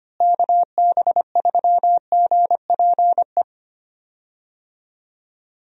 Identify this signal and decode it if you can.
Morse code, heard as a keyed tone, about 25 wpm, 700 Hz. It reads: KB3GPE
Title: USA Call Signs - 6 Characters - Repeated 2x - 25wpm